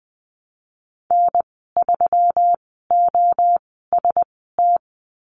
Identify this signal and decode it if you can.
Morse code, heard as a keyed tone, about 20 wpm, spaced as for 15 wpm, 700 Hz. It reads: N3OST
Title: 122 Lesson 20g - Number 3 - Call Signs 20wpm